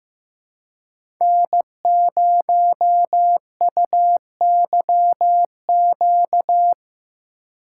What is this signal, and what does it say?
Morse code, heard as a keyed tone, about 15 wpm, 700 Hz. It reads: N0UYQ
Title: USA Call Signs - 5 Characters - 4x Character Spacing - 15wpm